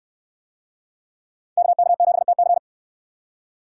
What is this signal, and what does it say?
Morse code, heard as a keyed tone, about 69 wpm, 700 Hz. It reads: KK7TC